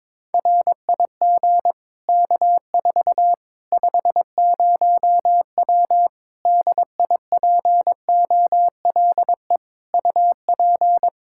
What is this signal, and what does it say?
Morse code, heard as a keyed tone, about 22 wpm, 700 Hz. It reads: RIG K4 50W DIPOLE UP
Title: Practice QSO - Exchange 2 - Info - 22wpm